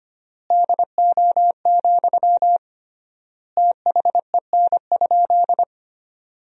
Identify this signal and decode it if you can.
Morse code, heard as a keyed tone, about 25 wpm, 700 Hz. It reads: DO, THEN?
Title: 3x Word-Spacing - Sentences from Top 100 Words - 25wpm